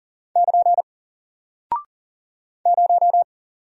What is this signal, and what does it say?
Morse code, heard as a keyed tone, about 39 wpm, 700 Hz. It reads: <KN> 0
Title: Single Character (Letter:Number:Punctuation:Prosign) 40wpm